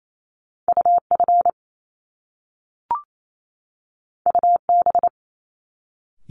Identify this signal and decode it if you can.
Morse code, heard as a keyed tone, about 28 wpm, 700 Hz. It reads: UF UB
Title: Top Two Letter Combinations in Words 28wpm